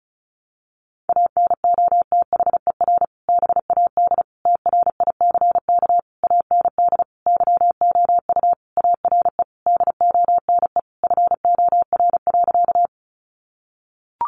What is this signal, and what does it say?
Morse code, heard as a keyed tone, about 34 wpm, 700 Hz. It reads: ANOTHER BAD TRICK AND YOU ARE DONE FOR.
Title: Sentences from Top 1500 Words 35wpm